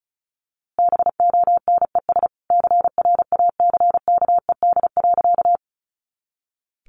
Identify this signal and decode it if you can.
Morse code, heard as a keyed tone, about 34 wpm, 700 Hz. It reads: BONES CRACKED.
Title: A Sound Of Thunder 35wpm